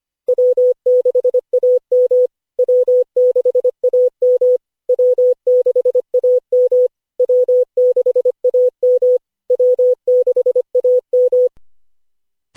Morse code, heard as a keyed tone, about 25 wpm, 500 Hz. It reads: W6AM W6AM W6AM W6AM W6AM
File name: cw / Session5 cw training